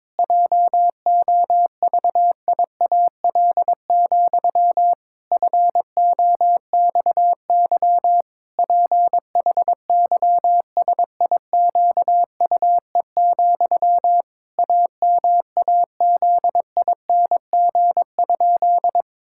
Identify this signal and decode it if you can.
Morse code, heard as a keyed tone, about 22 wpm, 700 Hz. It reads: JOVIAL, FOXY PHYSIQUE, AMAZING?
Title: Pangram Challenge v2 - 22wpm